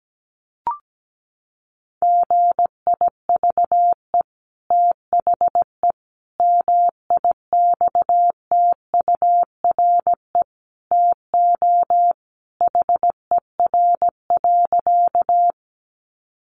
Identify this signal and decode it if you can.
Morse code, heard as a keyed tone, about 17 wpm, 700 Hz. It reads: GIVE THE MIXTURE TO HER.
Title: Sentences from Top 1500 Words 17wpm